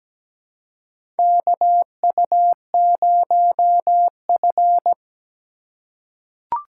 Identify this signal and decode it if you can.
Morse code, heard as a keyed tone, about 17 wpm, 700 Hz. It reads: KU0F
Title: USA Call Signs - 4 Characters - 4x Character Spacing - 17wpm